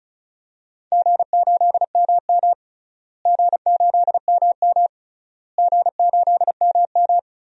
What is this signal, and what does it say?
Morse code, heard as a keyed tone, about 34 wpm, 700 Hz. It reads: G8MM G8MM G8MM
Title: DXCC-100 Call Signs - 4 Characters - Repeated 3x - 35wpm